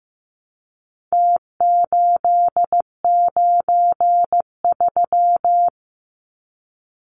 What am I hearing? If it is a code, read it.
Morse code, heard as a keyed tone, about 15 wpm, 700 Hz. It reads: T893
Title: Numbers - Contest Style - 4-Digit 15wpm